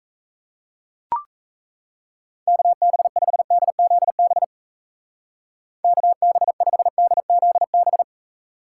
Morse code, heard as a keyed tone, about 33 wpm, 700 Hz. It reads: KB5DZB KB5DZB
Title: Speed Racing - USA 2x3 Call Signs 28wpm